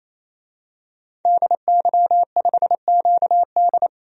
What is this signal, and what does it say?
Morse code, heard as a keyed tone, about 28 wpm, 700 Hz. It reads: DY5QD
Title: DXCC-100 Call Signs - 5 Characters - 28wpm